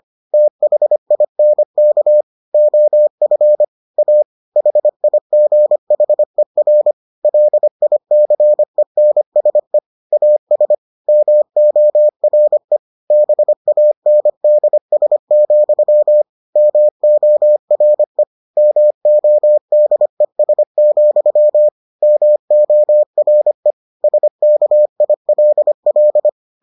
Morse code, heard as a keyed tone, about 25 wpm, 600 Hz. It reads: THINK OF A HIGHER LICENSE AS MORE BANDS, MORE MODES, MORE SKILL